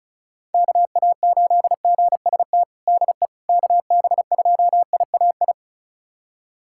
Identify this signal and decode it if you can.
Morse code, heard as a keyed tone, about 36 wpm, 700 Hz. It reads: KA8GST DE KB2IAI
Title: USA Callsign DE Callsign - 6 Characters - 35wpm